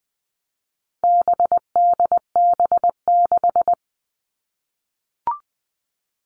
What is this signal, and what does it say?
Morse code, heard as a keyed tone, about 20 wpm, 700 Hz. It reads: BDB6
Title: All Cut Numbers - 4-Digit 20wpm